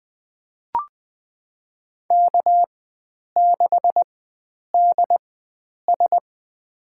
Morse code, heard as a keyed tone, about 20 wpm, spaced as for 9 wpm, 700 Hz. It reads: K6DS
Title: USA Call Signs - 4 Characters - 4x Character Spacing - 20wpm